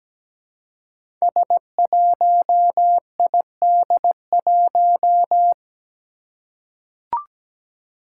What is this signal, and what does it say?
Morse code, heard as a keyed tone, about 17 wpm, 700 Hz. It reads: S1ID1
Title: Five Challenging Letter-Number Combinations 17wpm